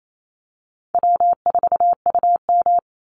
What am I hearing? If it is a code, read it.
Morse code, heard as a keyed tone, about 28 wpm, 700 Hz. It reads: W4UM
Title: Easy Call Signs - 28wpm